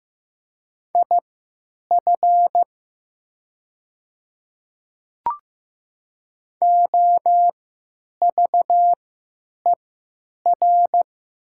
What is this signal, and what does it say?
Morse code, heard as a keyed tone, about 15 wpm, spaced as for 9 wpm, 700 Hz. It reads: IF OVER
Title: 3x Character Spacing - Top 200 Words - 15wpm